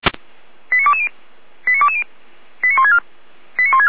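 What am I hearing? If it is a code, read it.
Five-tone selective call (ZVEI), heard as quick runs of five tones, tones of about 80 ms. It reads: 89220 89220 89164 89164
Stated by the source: ZVEI five-tone selective call 89025 received by SDR.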